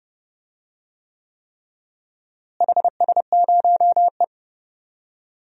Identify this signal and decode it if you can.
Morse code, heard as a keyed tone, about 30 wpm, 700 Hz. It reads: HS0E
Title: DXCC-100 Call Signs - 4 Characters - Repeated 3x - 30wpm